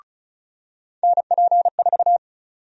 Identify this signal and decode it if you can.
Morse code, heard as a keyed tone, about 36 wpm, 700 Hz. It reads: NP4
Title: USA Call Sign Prefixes - 35wpm